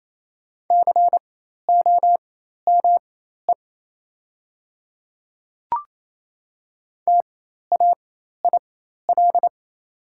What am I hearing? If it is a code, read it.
Morse code, heard as a keyed tone, about 28 wpm, spaced as for 13 wpm, 690 Hz. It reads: COME TAIL
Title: 4x Character Spacing - Top 500 Words - 28wpm